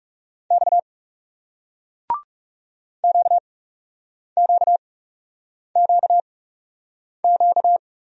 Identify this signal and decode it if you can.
Morse code, heard as a keyed tone, about 40 wpm, spaced as for 8 wpm, 700 Hz. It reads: X QQQQ